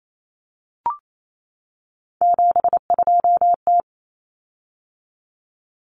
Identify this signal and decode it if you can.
Morse code, heard as a keyed tone, about 28 wpm, 700 Hz. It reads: 72T
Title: Common Cut Numbers - 3-Digit 28wpm